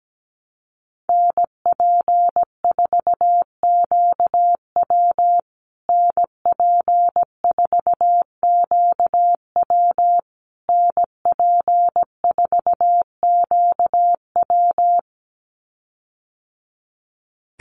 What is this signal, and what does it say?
Morse code, heard as a keyed tone, about 17 wpm, 700 Hz. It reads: NP4QW NP4QW NP4QW
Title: USA Call Signs Repeated 3x 17wpm